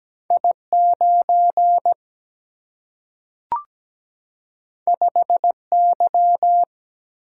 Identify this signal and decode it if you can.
Morse code, heard as a keyed tone, about 17 wpm, 700 Hz. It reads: I 9 5 Y
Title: Single-Letter Single-Number 17wpm